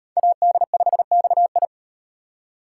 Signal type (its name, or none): Morse code